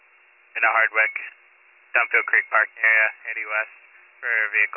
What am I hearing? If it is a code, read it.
Emergency Alert System (SAME) tones, heard as harsh buzzing bursts, with speech. It is heard from 0.7 to 3.7 s.